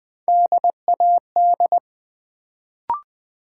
Morse code, heard as a keyed tone, about 20 wpm, 700 Hz. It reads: DAD
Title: Build A Word - 1 Letter At A Time - 3-Letter Words - 20wpm